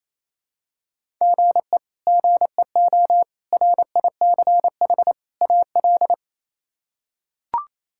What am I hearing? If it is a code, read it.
Morse code, heard as a keyed tone, about 28 wpm, 700 Hz. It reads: GE GEO RICH AL
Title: K1USN SST QSO Practice - 28wpm